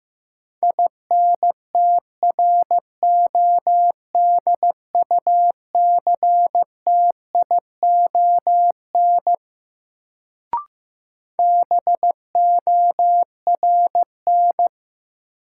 Morse code, heard as a keyed tone, about 15 wpm, 700 Hz. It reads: INTRODUCTION BORN
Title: Sentences from Top 1000 Words - New Word Drill - 15wpm